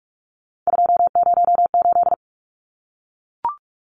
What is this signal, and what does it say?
Morse code, heard as a keyed tone, about 44 wpm, 700 Hz. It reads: 208
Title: Numbers - 3-Digit 45wpm